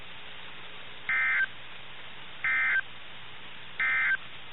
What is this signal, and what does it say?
Emergency Alert System (SAME) tones, heard as harsh buzzing bursts. It is heard from 1.0 to 4.2 s.
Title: NOAA Weather Radio - KIH61